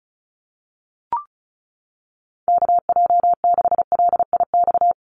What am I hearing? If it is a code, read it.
Morse code, heard as a keyed tone, about 34 wpm, 700 Hz. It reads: KJ6LIX
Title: USA Call Signs 35wpm